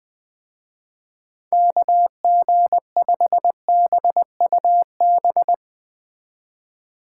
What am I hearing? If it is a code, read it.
Morse code, heard as a keyed tone, about 22 wpm, 700 Hz. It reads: KG5BUB
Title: Speed Racing - USA Call Signs 20wpm